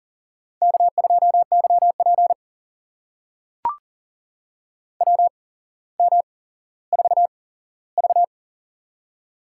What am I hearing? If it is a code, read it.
Morse code, heard as a keyed tone, about 40 wpm, spaced as for 17 wpm, 700 Hz. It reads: K2YP WM4V
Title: USA Call Signs - 4 Characters - 8x Character Spacing - 40wpm